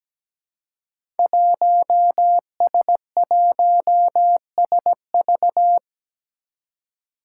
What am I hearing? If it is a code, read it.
Morse code, heard as a keyed tone, about 17 wpm, 700 Hz. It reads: 1S1SV
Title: Five Challenging Letter-Number Combinations 17wpm